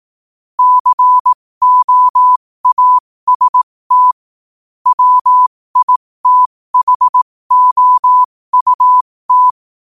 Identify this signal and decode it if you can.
Morse code, heard as a keyed tone, about 18 wpm, spaced as for 15 wpm, 1000 Hz. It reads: COAST WITHOUT
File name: qotdfiles / 15wpm